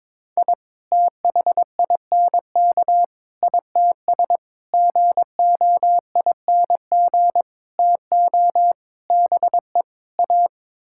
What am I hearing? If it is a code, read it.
Morse code, heard as a keyed tone, about 22 wpm, 700 Hz. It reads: I THINK ITS GOING TO BE A
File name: practice / 22wpm / Sets of 7 Words - 22wpm